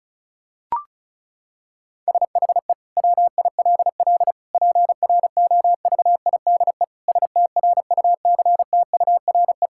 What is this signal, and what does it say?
Morse code, heard as a keyed tone, about 34 wpm, 700 Hz. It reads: SHE WILL PROVIDE STRUCTURE
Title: Sentences from Top 1000 Words - Review Part 6 - 35wpm